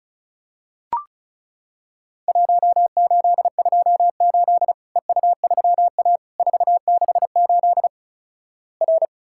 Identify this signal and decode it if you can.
Morse code, heard as a keyed tone, about 36 wpm, 700 Hz. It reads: 1828 EU3A 468 R
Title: Worked All Europe DX Contest - QTC Exchange - 35wpm